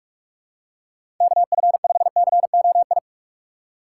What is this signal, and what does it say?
Morse code, heard as a keyed tone, about 44 wpm, 700 Hz. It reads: KF5COI
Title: USA Call Signs - 6 Characters - 6x Character Spacing - 45wpm